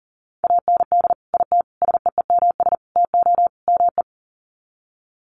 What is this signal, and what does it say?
Morse code, heard as a keyed tone, about 40 wpm, 700 Hz. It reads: AND IT SEEMS TO ME
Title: Sets of 5 Words Encore2 40wpm